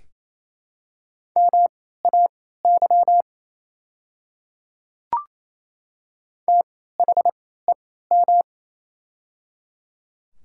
Morse code, heard as a keyed tone, about 28 wpm, spaced as for 16 wpm, 700 Hz. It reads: MAY THEM
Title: 3x Character Spacing - Top 100 Words - 28wpm